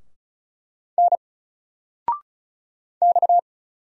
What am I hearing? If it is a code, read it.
Morse code, heard as a keyed tone, about 33 wpm, 700 Hz. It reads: N X
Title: Single Letter-Number 35wpm